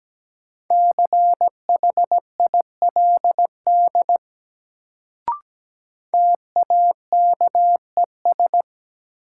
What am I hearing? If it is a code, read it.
Morse code, heard as a keyed tone, about 17 wpm, 700 Hz. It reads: CHILD TAKES
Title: Sets of 5 Words Encore - New Word Drill - 17wpm